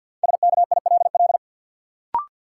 Morse code, heard as a keyed tone, about 52 wpm, 700 Hz. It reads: SKILL
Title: Sentences from Top 1200 Words - New Word Drill Alt - 50wpm